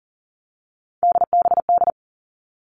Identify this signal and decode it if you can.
Morse code, heard as a keyed tone, about 40 wpm, 700 Hz. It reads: DBD
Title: All Cut Numbers - 3-Digit 40wpm